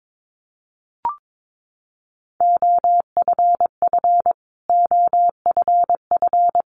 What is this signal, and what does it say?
Morse code, heard as a keyed tone, about 22 wpm, 700 Hz. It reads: OFF OFF
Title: Top 500 Words Repeated 22wpm